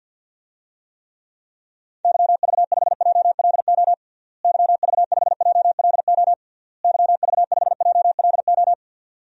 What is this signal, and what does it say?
Morse code, heard as a keyed tone, about 52 wpm, 700 Hz. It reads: YV5JLO YV5JLO YV5JLO
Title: DXCC-100 Call Signs - 6 Characters - Repeated 3x - 50wpm